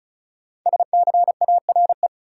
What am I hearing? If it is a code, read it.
Morse code, heard as a keyed tone, about 34 wpm, 700 Hz. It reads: SCARE
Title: Sentences from Top 1000 Words - Review Part 5 - New Word Drill - 35wpm